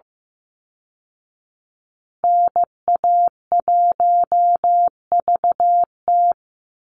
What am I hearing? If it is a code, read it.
Morse code, heard as a keyed tone, about 15 wpm, 700 Hz. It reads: NA1VT
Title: Call Signs Repeated 3x - 15wpm